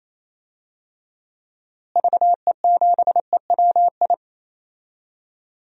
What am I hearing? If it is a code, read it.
Morse code, heard as a keyed tone, about 28 wpm, 700 Hz. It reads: VE7EWI
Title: DXCC-100 Call Signs - 6 Characters - Repeated 2x - 28wpm